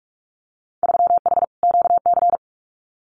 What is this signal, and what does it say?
Morse code, heard as a keyed tone, about 44 wpm, 700 Hz. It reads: 3H QC
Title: Winter Field Day Exchanges 45wpm